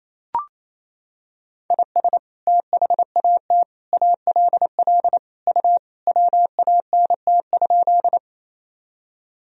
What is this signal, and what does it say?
Morse code, heard as a keyed tone, about 28 wpm, 700 Hz. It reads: IS THAT ALL U WANT?
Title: Sentences from Top 200 Words and Abbreviations 28wpm